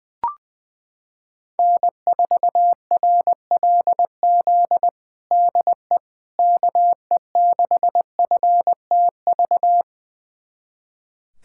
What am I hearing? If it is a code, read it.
Morse code, heard as a keyed tone, about 21 wpm, 700 Hz. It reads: N4RLZ DE KE6FTV
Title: USA Callsign DE Callsign 20wpm